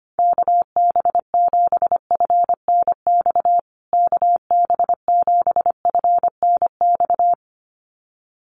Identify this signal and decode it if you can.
Morse code, heard as a keyed tone, about 25 wpm, 700 Hz. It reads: KB7FNX KB7FNX
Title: USA Call Signs Repeated 2x 25wpm